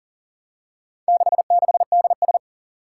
Morse code, heard as a keyed tone, about 39 wpm, 700 Hz. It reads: 66DS
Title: Four Challenging Letter-Number Combinations 40wpm